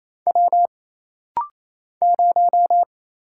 Morse code, heard as a keyed tone, about 28 wpm, 700 Hz. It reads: W 0
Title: Single Letter-Number Alt - Rapid-Fire 28wpm